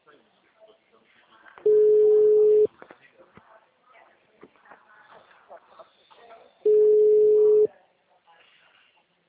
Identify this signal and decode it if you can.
Ringback tone (Europe, 425 Hz), with speech.
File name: out- -2007- - - .82